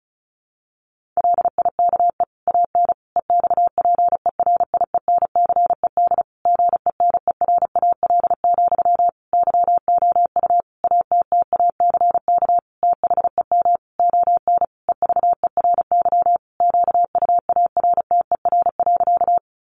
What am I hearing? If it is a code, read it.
Morse code, heard as a keyed tone, about 34 wpm, 700 Hz. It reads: LIKE AN EXPERIENCED GENERAL, YOU ATTACK THEM ON EVERY QUARTER.
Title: Sentences from Top 1500 Words 35wpm